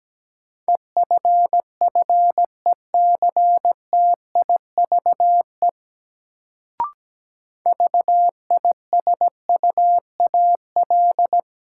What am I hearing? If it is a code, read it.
Morse code, heard as a keyed tone, about 17 wpm, 700 Hz. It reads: EFFECTIVE VISUAL
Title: Sentences from Top 1000 Words - Review Part 6 - New Word Drill - 17wpm